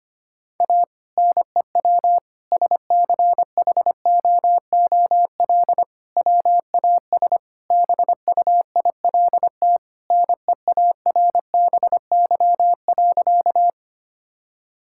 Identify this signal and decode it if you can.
Morse code, heard as a keyed tone, about 25 wpm, 700 Hz. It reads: A NEW SCHOOL WAS BUILT NEARBY.
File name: practice / 25wpm / Sentences with 2-Word Phrases - 25wpm